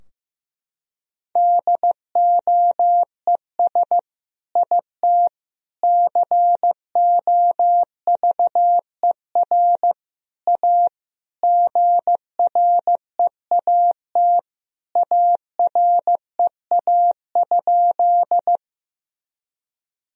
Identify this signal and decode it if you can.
Morse code, heard as a keyed tone, about 15 wpm, 700 Hz. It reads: DOES IT COVER A GREAT AREA?
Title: Sentences from Top 1000 Words - Review Part 2 - 15wpm